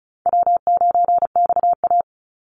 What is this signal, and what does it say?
Morse code, heard as a keyed tone, about 34 wpm, 700 Hz. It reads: W9XA
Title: Call Signs Repeated 2x - 35wpm